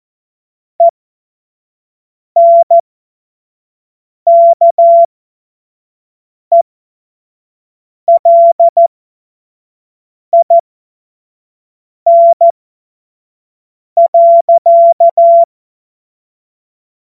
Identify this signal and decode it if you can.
Morse code, heard as a keyed tone, about 14 wpm, spaced as for 5 wpm, 680 Hz. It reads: ENKELIN.
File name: Morsekurs / Witze / Witz 20. Dezember 2022 25 F70